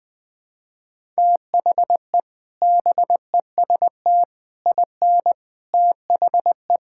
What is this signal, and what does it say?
Morse code, heard as a keyed tone, about 20 wpm, 700 Hz. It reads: THE BEST IN THE
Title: Sentences with 3-Word Phrases - Spoken Beginning - 20wpm